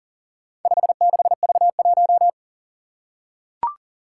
Morse code, heard as a keyed tone, about 40 wpm, 700 Hz. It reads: 56V1
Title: Four Challenging Letter-Number Combinations 40wpm